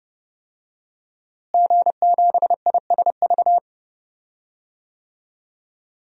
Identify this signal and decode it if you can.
Morse code, heard as a keyed tone, about 30 wpm, 700 Hz. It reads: G7ISV
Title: DXCC-100 Call Signs - 5 Characters - Repeated 2x - 30wpm